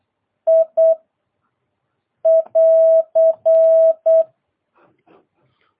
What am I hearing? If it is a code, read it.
Morse code, heard as a keyed tone, about 8 wpm, 650 Hz. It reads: I <AR>